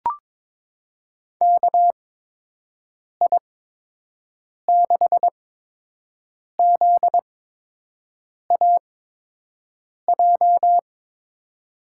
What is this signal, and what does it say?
Morse code, heard as a keyed tone, about 22 wpm, spaced as for 6 wpm, 700 Hz. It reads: KI6ZAJ
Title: USA Call Signs - 6 Characters - 8x Character Spacing - 22wpm